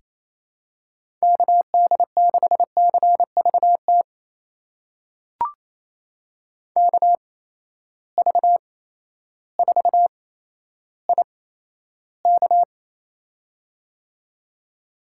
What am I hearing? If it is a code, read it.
Morse code, heard as a keyed tone, about 28 wpm, spaced as for 13 wpm, 700 Hz. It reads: KD6CVT KV4IK
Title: USA Call Signs - 8x Character Spacing - 28wpm